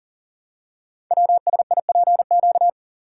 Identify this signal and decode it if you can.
Morse code, heard as a keyed tone, about 39 wpm, 700 Hz. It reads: WSIPQ
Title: Code Groups - 5 Letters - 40wpm